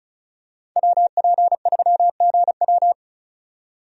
Morse code, heard as a keyed tone, about 36 wpm, 700 Hz. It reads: WP3GW
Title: USA Call Signs - 5 Characters - 35wpm